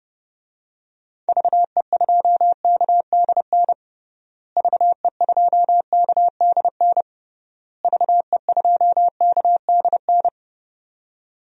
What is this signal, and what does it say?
Morse code, heard as a keyed tone, about 30 wpm, 700 Hz. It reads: VE2KDN VE2KDN VE2KDN
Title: DXCC-100 Call Signs - 6 Characters - Repeated 3x - 30wpm